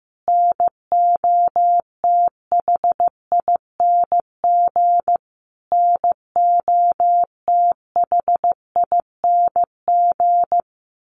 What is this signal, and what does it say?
Morse code, heard as a keyed tone, about 15 wpm, 700 Hz. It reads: NOTHING NOTHING
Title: Top 500 Words Repeated 15wpm